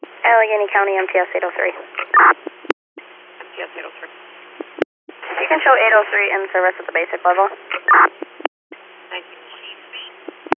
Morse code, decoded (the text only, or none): none